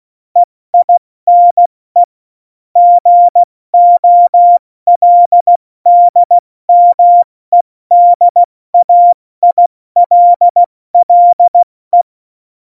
Morse code, heard as a keyed tone, about 16 wpm, 700 Hz. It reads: EINE GOLDMEDAILLE
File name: Morsekurs / Witze / Witz 01. Dezember 2020, 70 F80